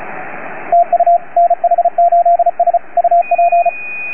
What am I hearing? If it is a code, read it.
Morse code, heard as a keyed tone, about 34 wpm, 690 Hz, with radio noise. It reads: TU NH8S UP